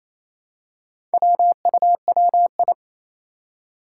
Morse code, heard as a keyed tone, about 27 wpm, 700 Hz. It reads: WUWI
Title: Four Challenging Letter-Number Combinations 28wpm